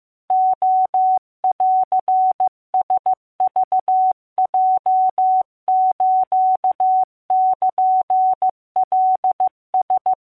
Morse code (Japanese Wabun code), heard as a keyed tone, about 15 wpm, 750 Hz. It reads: レンラクヲスルカラ